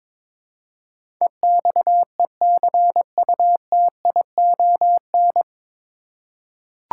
Morse code, heard as a keyed tone, about 22 wpm, 700 Hz. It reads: EXECUTION
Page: Morse Code Ninja practice 22wpm